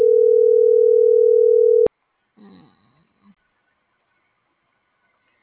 Ringback tone (North America), 440 + 480 Hz.